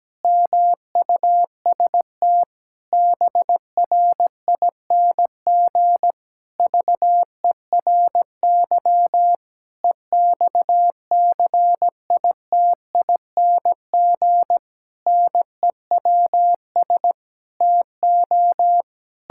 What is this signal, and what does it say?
Morse code, heard as a keyed tone, about 17 wpm, 700 Hz. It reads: MUST BRING VERY EXCITING NEWS TO
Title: Pangram Challenge v2 - 17wpm